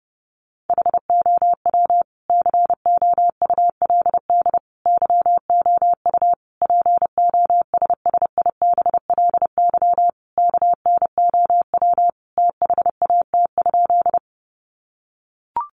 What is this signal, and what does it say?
Morse code, heard as a keyed tone, about 30 wpm, 700 Hz. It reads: HOW COULD YOU POSSIBLY KNOW THAT?
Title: Sentences from Top 1300 Words 30wpm